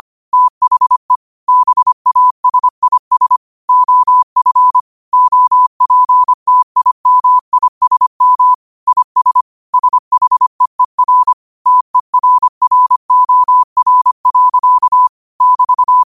Morse code, heard as a keyed tone, about 25 wpm, 1000 Hz. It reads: THE BASIS OF OPTIMISM IS SHEER TERROR. <BT>